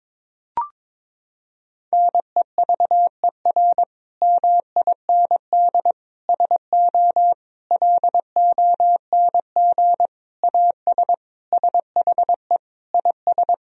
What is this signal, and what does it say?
Morse code, heard as a keyed tone, about 22 wpm, 700 Hz. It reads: NEVER MIND SO LONG AS SHE IS
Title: Sentences from Top 1000 Words - Review Part 1 - 22wpm